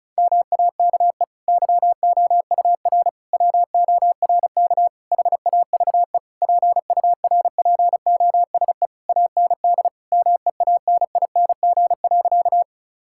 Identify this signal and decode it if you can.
Morse code, heard as a keyed tone, about 34 wpm, 700 Hz. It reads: MAKE YOUR WORK HAVE PURPOSE AND MEANING.
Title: Sentences from Top 1000 Words - Review Part 6 - 35wpm